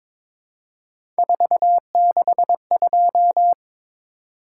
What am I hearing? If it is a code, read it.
Morse code, heard as a keyed tone, about 23 wpm, 700 Hz. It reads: 462
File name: practice / 22wpm / Three Letter-Number Combinations - 22wpm